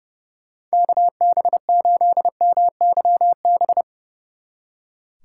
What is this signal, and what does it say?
Morse code, heard as a keyed tone, about 37 wpm, 700 Hz. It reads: KB8MYB
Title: Speed Racing - USA Call Signs 30wpm